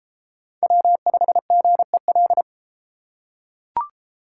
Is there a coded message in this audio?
Yes: Morse code from 0.6 to 2.4 s, heard as a keyed tone, about 32 wpm, 700 Hz. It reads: W5GEL